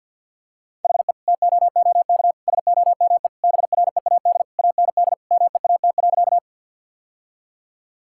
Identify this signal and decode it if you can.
Morse code, heard as a keyed tone, about 50 wpm, 700 Hz. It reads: HE TOOK SOME BREAD AND MEAT.
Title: Sentences from Top 1600 Words 50wpm